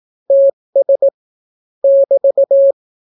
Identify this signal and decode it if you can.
Morse code, heard as a keyed tone, about 18 wpm, 550 Hz. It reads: TS <BT>